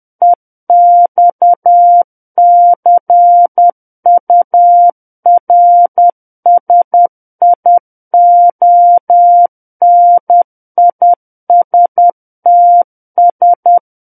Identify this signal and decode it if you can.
Morse code, heard as a keyed tone, about 10 wpm, 700 Hz. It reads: EXCURSIONISTS
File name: Book One / 01-Chapter 1 - 10 WPM